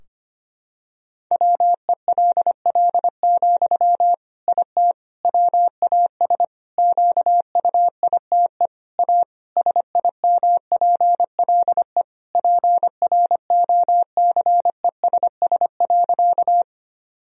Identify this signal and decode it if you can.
Morse code, heard as a keyed tone, about 25 wpm, 700 Hz. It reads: WELL, IT WAS QUITE A SIMPLE PROCESS.
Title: Sentences from Top 1000 Words - Review Part 3 - 25wpm